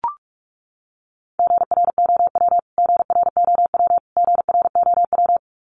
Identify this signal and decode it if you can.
Morse code, heard as a keyed tone, about 44 wpm, 700 Hz. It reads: GROW GROW GROW
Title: Top 500 Words Repeated 45wpm